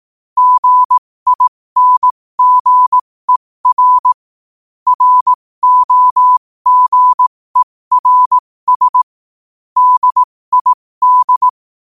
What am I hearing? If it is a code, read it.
Morse code, heard as a keyed tone, about 18 wpm, spaced as for 15 wpm, 1000 Hz. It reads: GINGER ROGERS DID